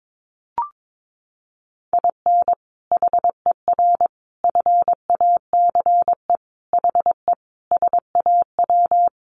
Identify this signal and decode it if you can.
Morse code, heard as a keyed tone, about 22 wpm, 700 Hz. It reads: IN HER FACE HE SAW
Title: Sentences from Top 1300 Words 22wpm